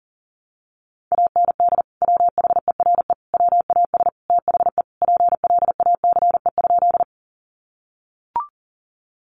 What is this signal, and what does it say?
Morse code, heard as a keyed tone, about 40 wpm, 700 Hz. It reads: AND WHERE WAS THE PLACE?
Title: Sentences from Top 200 Words 40wpm